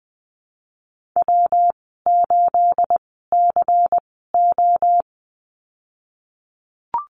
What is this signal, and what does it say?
Morse code, heard as a keyed tone, about 20 wpm, spaced as for 15 wpm, 700 Hz. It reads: W8CO
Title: 219 Lesson 35g - Number 0 - Call Signs 20wpm